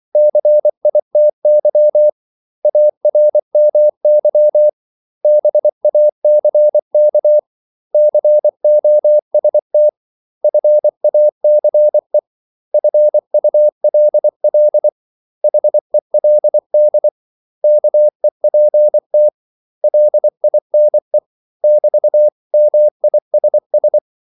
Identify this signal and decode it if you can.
Morse code, heard as a keyed tone, about 24 wpm, 600 Hz. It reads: CITY ARMY BACK COST FACE FULL HELD KEPT LINE <BT>MISS
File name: CW / Practice / 500MostCommonEnglishWords24